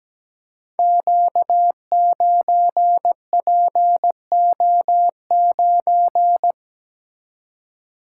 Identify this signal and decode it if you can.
Morse code, heard as a keyed tone, about 17 wpm, 690 Hz. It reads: Q9PO9
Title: Code Groups - 5 Letter:Numbers - 17wpm